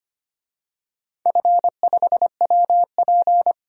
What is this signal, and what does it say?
Morse code, heard as a keyed tone, about 25 wpm, 700 Hz. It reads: F5WP
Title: DXCC-100 Call Signs - 4 Characters - 25wpm